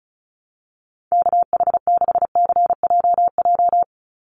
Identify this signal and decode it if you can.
Morse code, heard as a keyed tone, about 34 wpm, 700 Hz. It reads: KH6CJJ
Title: Call Signs Repeated 2x - 35wpm